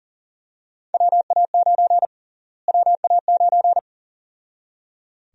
Morse code, heard as a keyed tone, about 38 wpm, 700 Hz. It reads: WA9 WA9
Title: USA Call Sign Prefixes - Repeated 2x - 40wpm